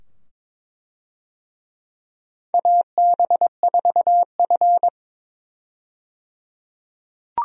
Morse code, heard as a keyed tone, about 22 wpm, 700 Hz. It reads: AB4F
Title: USA Call Signs - 4 Characters - 22wpm